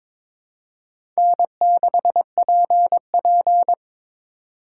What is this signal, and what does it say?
Morse code, heard as a keyed tone, about 21 wpm, 700 Hz. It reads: N6PP
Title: USA Call Signs - 4 Characters - 3x Character Spacing - 22wpm